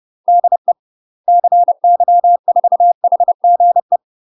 Morse code, heard as a keyed tone, about 30 wpm, 700 Hz. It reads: DE CY4HGE
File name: UK MASTER QSOs TRAINING / 30WPM / UK MASTER QSOs 0002